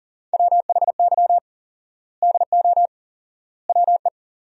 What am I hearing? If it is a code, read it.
Morse code, heard as a keyed tone, about 40 wpm, 700 Hz. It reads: WHY DO WE
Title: 4x Word-Spacing - Sentences from Top 400 Words - 40wpm